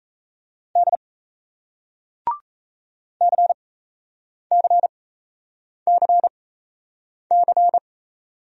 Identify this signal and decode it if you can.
Morse code, heard as a keyed tone, about 37 wpm, spaced as for 8 wpm, 700 Hz. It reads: D CCCC